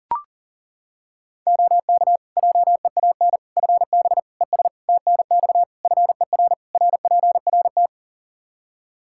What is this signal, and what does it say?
Morse code, heard as a keyed tone, about 40 wpm, 700 Hz. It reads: OK JEAN FB ES TNX FER RPRT
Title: Practice QSO - Exchange 2 - Info - 40wpm